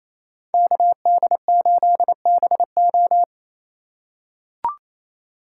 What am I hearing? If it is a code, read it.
Morse code, heard as a keyed tone, about 28 wpm, 700 Hz. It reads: KD8BO
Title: USA Call Signs - 4x Character Spacing - 28wpm